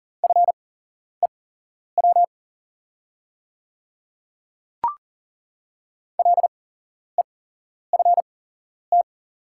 Morse code, heard as a keyed tone, about 39 wpm, spaced as for 11 wpm, 690 Hz. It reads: FEW LEFT
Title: 8x Character Spacing - Top 300 Words - 40wpm